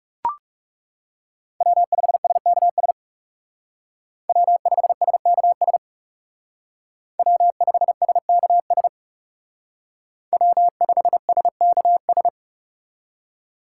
Morse code, heard as a keyed tone, about 36 wpm, 700 Hz. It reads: W5SKS W5SKS W5SKS W5SKS
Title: Speed Racing - USA Call Signs 30wpm